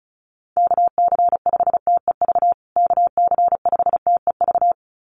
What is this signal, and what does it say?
Morse code, heard as a keyed tone, about 34 wpm, 700 Hz. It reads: KC5TEV KC5TEV
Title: USA Call Signs Repeated 2x 35wpm